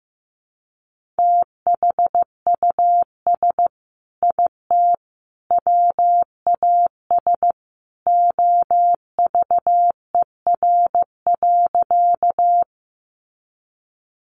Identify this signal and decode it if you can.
Morse code, heard as a keyed tone, about 15 wpm, 700 Hz. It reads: THUS IT WAS OVER.
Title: Sentences from Top 800 Words 15wpm